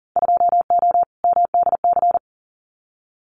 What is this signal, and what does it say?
Morse code, heard as a keyed tone, about 42 wpm, 700 Hz. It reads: 2O MDC
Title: Winter Field Day Exchanges 40wpm